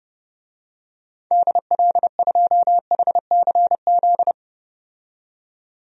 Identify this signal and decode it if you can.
Morse code, heard as a keyed tone, about 32 wpm, 700 Hz. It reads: DL2HCZ
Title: DXCC-100 Call Signs - 6 Characters - 30wpm